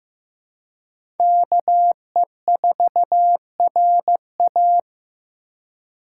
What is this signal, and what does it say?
Morse code, heard as a keyed tone, about 16 wpm, 700 Hz. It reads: KE4RA
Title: USA Call Signs - 5 Characters - 3x Character Spacing - 15wpm